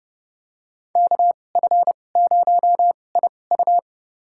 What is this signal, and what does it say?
Morse code, heard as a keyed tone, about 30 wpm, spaced as for 22 wpm, 700 Hz. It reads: KF0IU
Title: USA Call Signs - 5 Characters - 2x Character Spacing - 30wpm